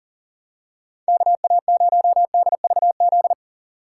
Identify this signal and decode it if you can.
Morse code, heard as a keyed tone, about 40 wpm, 700 Hz. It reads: KA0DVZ
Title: USA Call Signs - 6 Characters - 6x Character Spacing - 40wpm